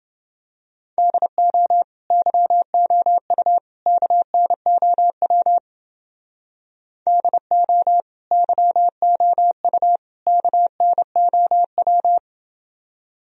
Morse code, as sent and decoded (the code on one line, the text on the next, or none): -.. --- / -.-- --- ..- / -.- -. --- .-- / -.. --- / -.-- --- ..- / -.- -. --- .--
DO YOU KNOW DO YOU KNOW